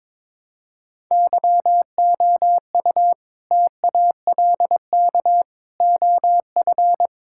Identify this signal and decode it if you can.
Morse code, heard as a keyed tone, about 22 wpm, 700 Hz. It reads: YOU TALK OF
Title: Sentences from Top 1600 Words 22wpm